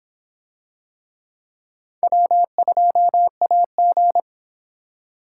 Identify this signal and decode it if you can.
Morse code, heard as a keyed tone, about 26 wpm, 700 Hz. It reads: W2AG